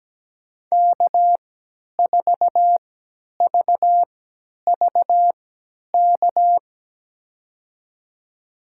Morse code, heard as a keyed tone, about 17 wpm, spaced as for 10 wpm, 700 Hz. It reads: K4VVK
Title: USA Call Signs - 5 Characters - 3x Character Spacing - 17wpm